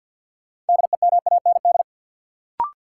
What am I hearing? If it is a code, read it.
Morse code, heard as a keyed tone, about 52 wpm, 700 Hz. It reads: DEMAND
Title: Sentences from Top 800 Words - New Word Drill - 50wpm